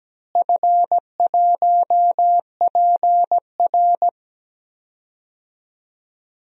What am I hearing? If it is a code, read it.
Morse code, heard as a keyed tone, about 17 wpm, 700 Hz. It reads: F1PR
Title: DXCC-100 Call Signs - 4 Characters - Repeated 3x - 17wpm